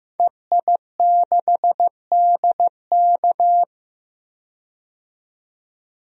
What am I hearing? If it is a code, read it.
Morse code, heard as a keyed tone, about 15 wpm, 700 Hz. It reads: EI6DK
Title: DXCC-100 Call Signs - 5 Characters - 15wpm